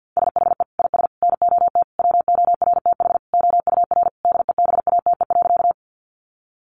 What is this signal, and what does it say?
Morse code, heard as a keyed tone, about 50 wpm, 700 Hz. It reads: SHE IS NOT WORTH OUR DEBATE.
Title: Sentences from Top 1100 Words 50wpm